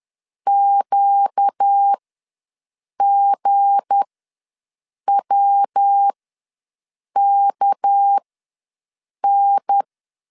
Morse code, heard as a keyed tone, about 10 wpm, spaced as for 6 wpm, 800 Hz. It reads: QGWKN
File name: LEZIONI / 7 GF9Z / 7 G ws 10wpm 3843